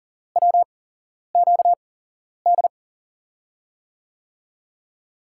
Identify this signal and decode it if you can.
Morse code, heard as a keyed tone, about 40 wpm, spaced as for 11 wpm, 700 Hz. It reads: WQD
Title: USA Call Signs - 8x Character Spacing - 40wpm